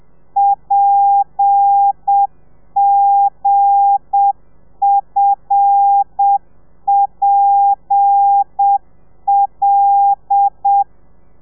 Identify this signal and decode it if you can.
Morse code, heard as a keyed tone, about 7 wpm, 800 Hz. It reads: PGFPL
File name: telegrafia / i2tfj / Lez8 30